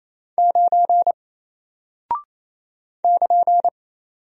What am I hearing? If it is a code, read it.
Morse code, heard as a keyed tone, about 27 wpm, 700 Hz. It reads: 9 <KN>
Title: Single Character (Letter:Number:Punctuation:Prosign) 28wpm